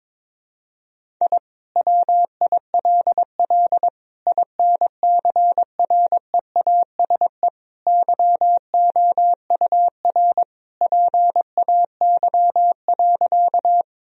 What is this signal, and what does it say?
Morse code, heard as a keyed tone, about 22 wpm, 700 Hz. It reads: I WILL INCREASE YOUR PAY.
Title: Sentences from Top 1000 Words - Review Part 4 - 22wpm